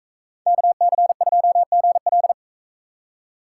Morse code, heard as a keyed tone, about 41 wpm, 700 Hz. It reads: KC2GL